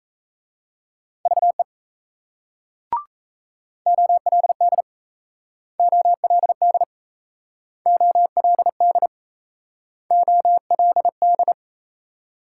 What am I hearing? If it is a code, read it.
Morse code, heard as a keyed tone, about 37 wpm, 700 Hz. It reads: VE OLD OLD OLD OLD